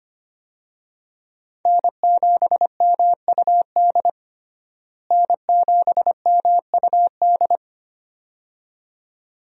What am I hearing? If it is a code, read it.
Morse code, heard as a keyed tone, about 25 wpm, 690 Hz. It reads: N7MUD N7MUD
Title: USA Call Signs - 5 Characters - Repeated 2x - 25wpm